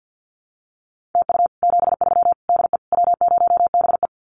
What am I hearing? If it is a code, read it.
Morse code, heard as a keyed tone, about 50 wpm, 700 Hz. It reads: TU 73 DE W0BE
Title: Common QSO Phrases Encore 50wpm